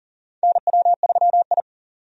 Morse code, heard as a keyed tone, about 40 wpm, 690 Hz. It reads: NW3I
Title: USA Call Signs - 4 Characters - 40wpm